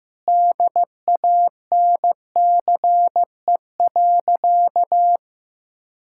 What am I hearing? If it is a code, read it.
Morse code, heard as a keyed tone, about 16 wpm, 700 Hz. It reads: DANCE.
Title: Sentences from Top 1000 Words - Review Part 6 - 15wpm